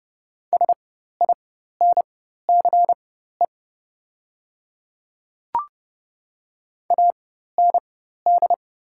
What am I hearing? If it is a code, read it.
Morse code, heard as a keyed tone, about 30 wpm, spaced as for 14 wpm, 700 Hz. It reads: SINCE AND
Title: 4x Character Spacing - Top 300 Words - 30wpm